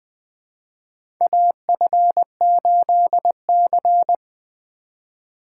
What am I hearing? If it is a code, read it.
Morse code, heard as a keyed tone, about 20 wpm, 700 Hz. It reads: AF8C
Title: USA Call Signs - 4 Characters - 2x Character Spacing - 20wpm